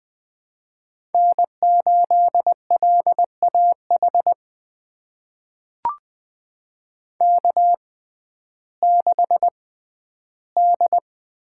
Morse code, heard as a keyed tone, about 20 wpm, spaced as for 12 wpm, 700 Hz. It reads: N8LAH K6D
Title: USA Call Signs - 5 Characters - 6x Character Spacing - 20wpm